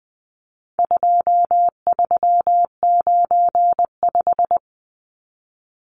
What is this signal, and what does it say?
Morse code, heard as a keyed tone, about 20 wpm, 700 Hz. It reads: 2395
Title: Numbers - 4-Digit 20wpm